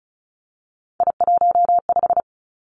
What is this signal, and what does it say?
Morse code, heard as a keyed tone, about 36 wpm, 700 Hz. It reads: I15
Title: Three Challenging Letter-Number Combinations 35wpm